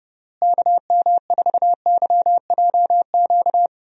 Morse code, heard as a keyed tone, about 30 wpm, 700 Hz. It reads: KM4YJQ
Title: USA Call Signs - 6 Characters - Repeated 3x - 30wpm